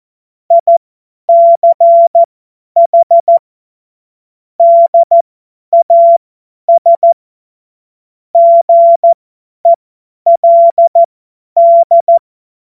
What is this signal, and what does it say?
Morse code, heard as a keyed tone, about 14 wpm, spaced as for 10 wpm, 680 Hz. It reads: ICH DAS GELD
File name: Morsekurs / Witze / Witz 21.November 2023 50 F70